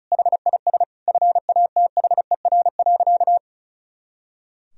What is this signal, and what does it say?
Morse code, heard as a keyed tone, about 36 wpm, 700 Hz. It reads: HIS FATHER.
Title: Sentences from Top 500 Words and Abbreviations 35wpm